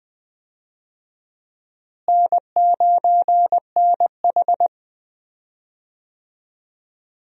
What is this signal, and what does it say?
Morse code, heard as a keyed tone, about 21 wpm, 700 Hz. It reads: N9NH
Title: USA Call Signs - 4 Characters - Repeated 3x - 20wpm